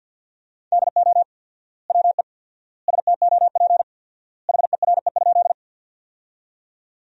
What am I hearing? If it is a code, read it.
Morse code, heard as a keyed tone, about 50 wpm, 700 Hz. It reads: DO WE STOP HERE?
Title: 4x Word-Spacing - Sentences from Top 300 Words - 50wpm